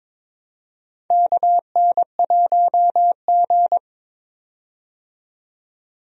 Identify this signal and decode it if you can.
Morse code, heard as a keyed tone, about 22 wpm, 700 Hz. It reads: KN1G
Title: USA Call Signs - 4 Characters - 22wpm